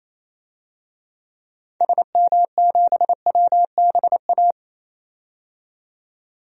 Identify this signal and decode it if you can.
Morse code, heard as a keyed tone, about 28 wpm, 700 Hz. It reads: SM7WBA
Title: DXCC-100 Call Signs - 6 Characters - 28wpm